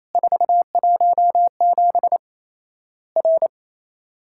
Morse code, heard as a keyed tone, about 28 wpm, 700 Hz. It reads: 417 R
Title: Worked All Europe DX Contest - QTC Exchange - 28wpm